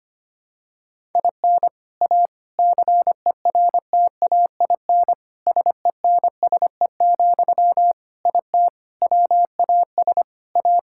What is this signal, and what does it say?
Morse code, heard as a keyed tone, about 25 wpm, 700 Hz. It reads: IN A CERTAIN SENSE, IT WAS A
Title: Sentences from Top 1000 Words - Review Part 6 - 25wpm